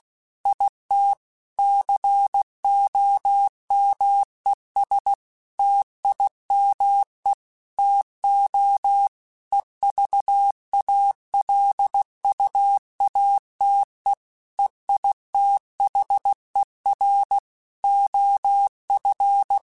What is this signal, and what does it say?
Morse code, heard as a keyed tone, about 16 wpm, 800 Hz. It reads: IT COMES TIME TO EVALUATE EITHER OF
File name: Technical Subjects / Technical Subjects-mp3s / AND Gate @15